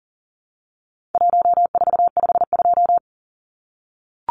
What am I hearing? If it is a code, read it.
Morse code, heard as a keyed tone, about 40 wpm, 700 Hz. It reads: 1452
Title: Numbers - Contest Style - 4-Digit 40wpm